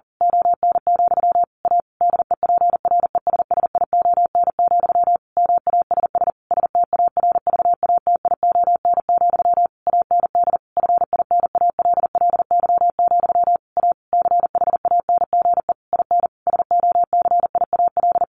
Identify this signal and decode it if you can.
Morse code, heard as a keyed tone, about 40 wpm, 700 Hz. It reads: ON, A DEPRESSION, MASS STARVATION, AND FINALLY, A CHANGE IN SOCIAL
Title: A Sound Of Thunder 40wpm